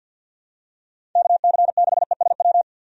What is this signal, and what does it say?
Morse code, heard as a keyed tone, about 47 wpm, 700 Hz. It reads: KC6ESW
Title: USA Call Signs - 6 Characters - Repeated 2x - 50wpm